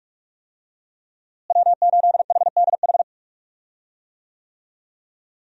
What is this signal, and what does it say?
Morse code, heard as a keyed tone, about 41 wpm, 700 Hz. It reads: W8HDH
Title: USA Call Signs - 5 Characters - Repeated 2x - 45wpm